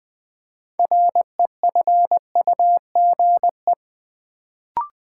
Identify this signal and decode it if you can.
Morse code, heard as a keyed tone, about 20 wpm, 700 Hz. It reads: REFUGE
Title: Build A Word - 3 Letters At A Time - 6-Letter Words - 20wpm